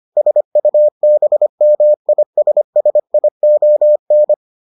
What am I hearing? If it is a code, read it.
Morse code, heard as a keyed tone, about 25 wpm, 600 Hz. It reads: SUBMISSION